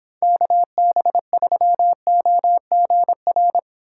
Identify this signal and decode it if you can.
Morse code, heard as a keyed tone, about 29 wpm, 700 Hz. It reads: KB3OGR